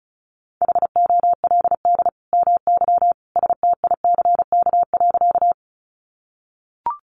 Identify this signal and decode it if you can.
Morse code, heard as a keyed tone, about 34 wpm, 700 Hz. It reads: HOLD MY STICK.
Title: Sentences from Top 1200 Words 35wpm